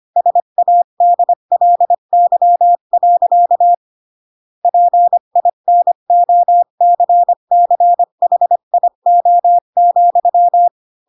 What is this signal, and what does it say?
Morse code, heard as a keyed tone, about 25 wpm, 700 Hz. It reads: SADLY. PINOCCHIO,
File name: Pinocchio / Pinocchio ch8 20 25 ik1hge